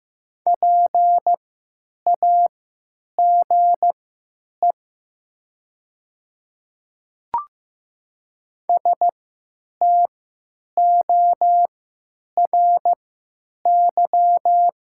Morse code, heard as a keyed tone, about 15 wpm, spaced as for 8 wpm, 700 Hz. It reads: PAGE STORY
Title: 3x Character Spacing - Top 300 Words - 15wpm